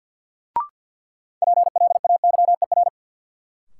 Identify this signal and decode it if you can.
Morse code, heard as a keyed tone, about 53 wpm, 700 Hz. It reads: PLAYER
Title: Sentences from Top 1000 Words - Review Part 4 - New Word Drill - 50wpm